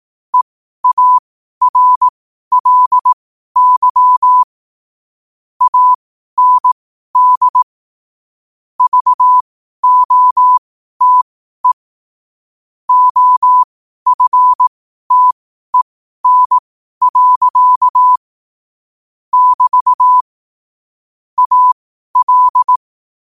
Morse code, heard as a keyed tone, about 18 wpm, spaced as for 12 wpm, 1000 Hz. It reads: EARLY AND VOTE OFTEN. <BT> AL
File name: qotdfiles / 12wpm